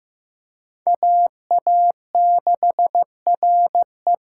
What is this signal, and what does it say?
Morse code, heard as a keyed tone, about 15 wpm, 700 Hz. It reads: AA6RE
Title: K1USN SST QSO Practice - 15wpm